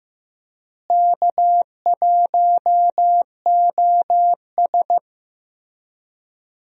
Morse code, heard as a keyed tone, about 15 wpm, 700 Hz. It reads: K1OS
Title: USA Call Signs - 4 Characters - Repeated 3x - 15wpm